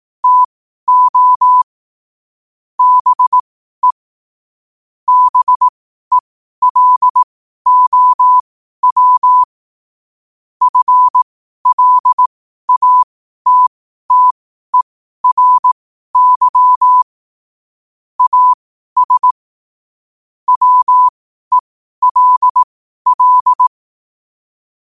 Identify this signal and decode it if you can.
Morse code, heard as a keyed tone, about 18 wpm, spaced as for 12 wpm, 1000 Hz. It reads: TO BE BELOW FLATTERY AS WELL
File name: qotdfiles / 12wpm